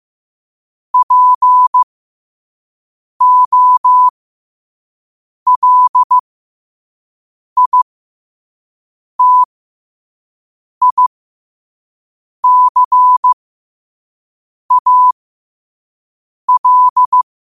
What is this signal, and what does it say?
Morse code, heard as a keyed tone, about 15 wpm, spaced as for 5 wpm, 1000 Hz. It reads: POLITICAL
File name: qotdfiles / 5wpm